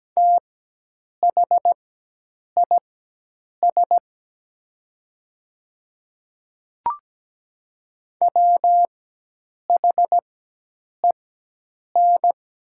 Morse code, heard as a keyed tone, about 17 wpm, spaced as for 8 wpm, 700 Hz. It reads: THIS WHEN
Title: 4x Character Spacing - Top 100 Words - 17wpm